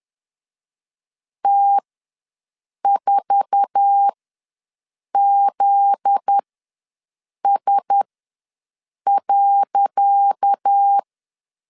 Morse code, heard as a keyed tone, about 10 wpm, spaced as for 6 wpm, 800 Hz. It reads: T4ZS.
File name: LEZIONI / 10 SIGNS / 10C DOT ws 10wpm 4213